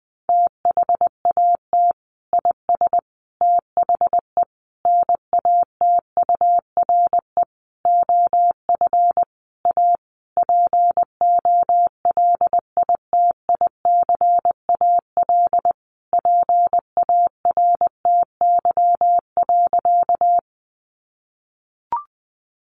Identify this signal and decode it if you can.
Morse code, heard as a keyed tone, about 20 wpm, 700 Hz. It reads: THAT IS THE NATURE OF A POLITICAL PARTY.
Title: Sentences from Top 1400 Words 20wpm